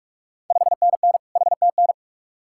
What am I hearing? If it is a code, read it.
Morse code, heard as a keyed tone, about 44 wpm, 690 Hz. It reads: 5NN HTN
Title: Ontario QSO Party Exchange 45wpm